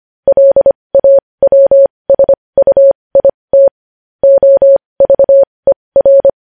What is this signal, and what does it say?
Morse code, heard as a keyed tone, about 25 wpm, spaced as for 20 wpm, 550 Hz. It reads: LAWSUIT OVER
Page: news-headlines-in-morse-code-at-20-wpm